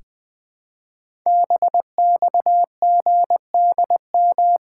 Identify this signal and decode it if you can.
Morse code, heard as a keyed tone, about 21 wpm, 700 Hz. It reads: BXGDM
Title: Code Groups - 5 Letters - 20wpm